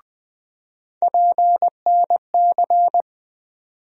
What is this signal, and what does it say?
Morse code, heard as a keyed tone, about 20 wpm, 700 Hz. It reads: PNC
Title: USA Call Sign Postfixes - 3 Letters 20wpm